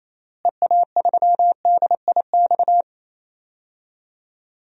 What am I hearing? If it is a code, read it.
Morse code, heard as a keyed tone, about 28 wpm, 700 Hz. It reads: EA3DIX
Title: DXCC-100 Call Signs - 6 Characters - Repeated 3x - 28wpm